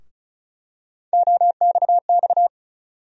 Morse code, heard as a keyed tone, about 34 wpm, 700 Hz. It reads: OXX
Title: USA Call Sign Postfixes - 3 Letters 35wpm